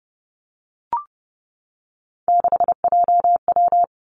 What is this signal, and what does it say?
Morse code, heard as a keyed tone, about 30 wpm, 700 Hz. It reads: 6JW
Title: Three Challenging Letter-Number Combinations 30wpm